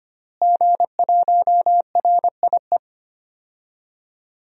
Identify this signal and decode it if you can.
Morse code, heard as a keyed tone, about 25 wpm, 700 Hz. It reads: G1RIE
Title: DXCC-100 Call Signs - 5 Characters - Repeated 3x - 25wpm